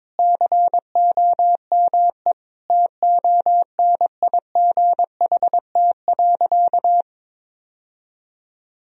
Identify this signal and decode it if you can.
Morse code, heard as a keyed tone, about 22 wpm, 700 Hz. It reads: COME TONIGHT.
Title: Sentences from Top 1000 Words - Review Part 3 - 22wpm